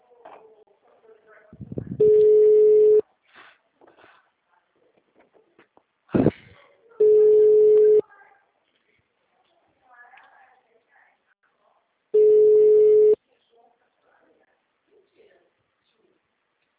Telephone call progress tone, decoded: ringback tone

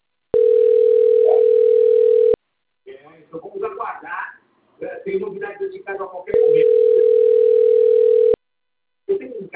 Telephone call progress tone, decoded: ringback tone

